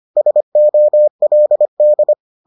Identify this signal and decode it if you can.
Morse code, heard as a keyed tone, about 25 wpm, 600 Hz. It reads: SOLD